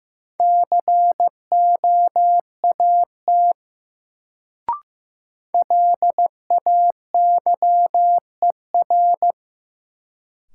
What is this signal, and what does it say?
Morse code, heard as a keyed tone, about 15 wpm, 700 Hz. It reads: COAT LAYER
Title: Sentences from Top 1600 Words - New Word Drill - 15wpm